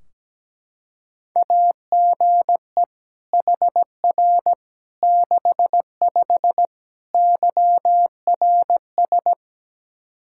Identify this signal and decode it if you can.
Morse code, heard as a keyed tone, about 17 wpm, 700 Hz. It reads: AGE HR 65 YRS
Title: Practice QSO - Exchange 3 - More Info - 17wpm